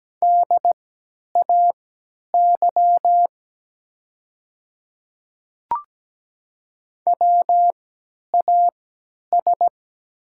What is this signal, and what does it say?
Morse code, heard as a keyed tone, about 17 wpm, spaced as for 10 wpm, 700 Hz. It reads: DAY WAS
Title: 3x Character Spacing - Top 100 Words - 17wpm